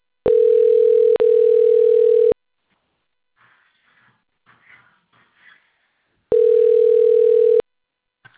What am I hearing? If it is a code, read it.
Ringback tone (North America), 440 + 480 Hz.
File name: rg-1600-unknown- - - .89826